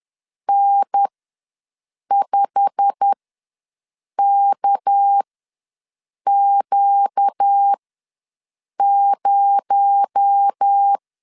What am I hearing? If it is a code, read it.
Morse code, heard as a keyed tone, about 10 wpm, spaced as for 6 wpm, 800 Hz. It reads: N5KQ0
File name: LEZIONI / 8 YJ45 / 8 YJ45 ws 10wpm 3843